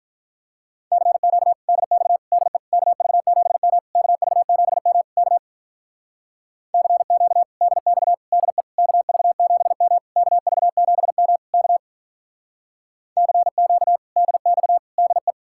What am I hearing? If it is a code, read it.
Morse code, heard as a keyed tone, about 46 wpm, 700 Hz. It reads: CQ DX DE KV7M KV7M K CQ DX DE KV7M KV7M K CQ DX DE